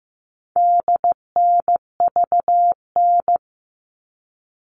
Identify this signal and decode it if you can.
Morse code, heard as a keyed tone, about 15 wpm, 700 Hz. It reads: DNVN
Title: All Cut Numbers - 4-Digit 15wpm